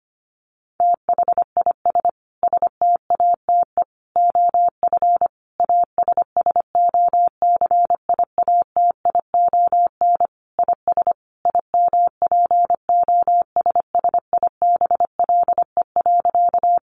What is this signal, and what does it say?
Morse code, heard as a keyed tone, about 25 wpm, 700 Hz. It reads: THIS STATE OF ASSOCIATION IS IMPOSSIBLE.
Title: Sentences from Top 1100 Words 25wpm